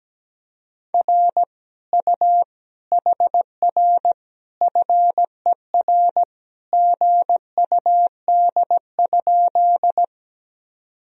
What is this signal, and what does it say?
Morse code, heard as a keyed tone, about 17 wpm, 700 Hz. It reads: R U HR FER GUD?
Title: Sentences from Top 200 Words and Abbreviations 17wpm